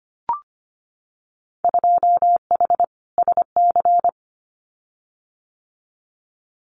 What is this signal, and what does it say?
Morse code, heard as a keyed tone, about 25 wpm, 690 Hz. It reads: 2H SC
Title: Winter Field Day Exchanges 25wpm